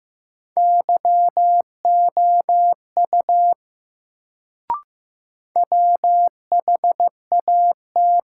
Morse code, heard as a keyed tone, about 15 wpm, 700 Hz. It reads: YOU WHAT
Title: Sentences from Top 100 Words - New Word Drill - 15wpm